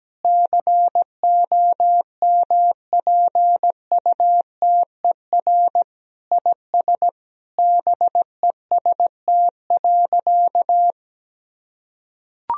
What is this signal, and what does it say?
Morse code, heard as a keyed tone, about 17 wpm, 700 Hz. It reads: COMPUTER IS BEST.
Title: Sentences from Top 1000 Words - Review Part 3 - 17wpm